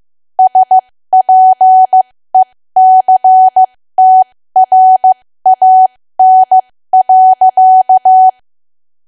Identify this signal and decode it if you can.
Morse code, heard as a keyed tone, about 15 wpm, spaced as for 13 wpm, 750 Hz. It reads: SPECTRAN.
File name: cwa / W1AW-Level2 / 13WPM